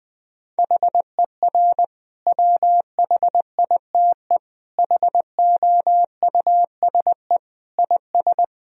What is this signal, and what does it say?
Morse code, heard as a keyed tone, about 19 wpm, 700 Hz. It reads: HER WHITE HOUSE IS
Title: Sentences from Top 1000 Words - Review Part 1 - 20wpm